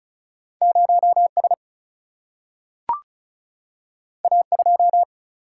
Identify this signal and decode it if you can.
Morse code, heard as a keyed tone, about 36 wpm, 700 Hz. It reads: 0S A2
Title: Single-Letter Single-Number 35wpm